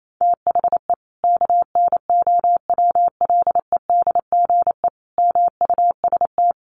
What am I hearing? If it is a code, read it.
Morse code, heard as a keyed tone, about 28 wpm, 700 Hz. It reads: THE KNOWLEDGE MUST
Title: Sentences from Top 1400 Words 28wpm